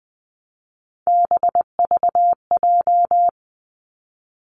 Morse code, heard as a keyed tone, about 20 wpm, 700 Hz. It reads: BVJ
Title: Three Challenging Letter-Number Combinations 20wpm